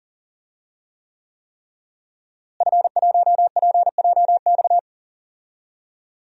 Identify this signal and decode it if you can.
Morse code, heard as a keyed tone, about 40 wpm, 700 Hz. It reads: F1PJX
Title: DXCC-100 Call Signs - 5 Characters - Repeated 3x - 40wpm